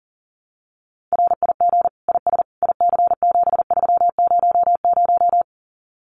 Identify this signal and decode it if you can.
Morse code, heard as a keyed tone, about 40 wpm, 700 Hz. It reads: RIG IS IC7300
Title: Common QSO Phrases Encore 40wpm